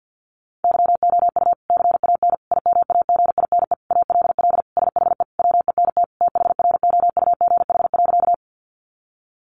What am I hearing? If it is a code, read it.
Morse code, heard as a keyed tone, about 50 wpm, 700 Hz. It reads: YOU CAN IMAGINE ALL SHE WENT THROUGH.
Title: Sentences from Top 1400 Words 50wpm